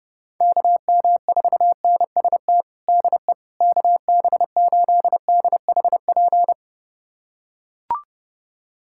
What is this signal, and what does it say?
Morse code, heard as a keyed tone, about 32 wpm, 700 Hz. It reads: KM4NST DE KB8DHP
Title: USA Callsign DE Callsign 30wpm